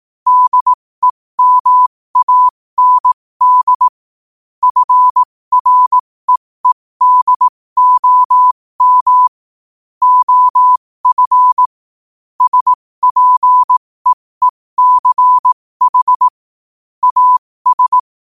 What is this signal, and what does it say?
Morse code, heard as a keyed tone, about 18 wpm, spaced as for 15 wpm, 1000 Hz. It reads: DEMAND FREEDOM OF SPEECH AS
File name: qotdfiles / 15wpm